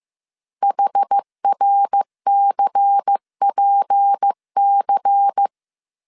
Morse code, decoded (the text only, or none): HRCPC